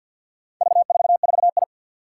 Morse code, heard as a keyed tone, about 53 wpm, 700 Hz. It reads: V44I
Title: Four Challenging Letter-Number Combinations 50wpm